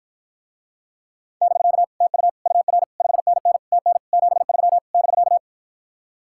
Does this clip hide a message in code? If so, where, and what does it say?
Yes: Morse code from 1.4 to 5.4 s, heard as a keyed tone, about 52 wpm, 700 Hz. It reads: <BK> TU UR 5NN TN 73 <BK>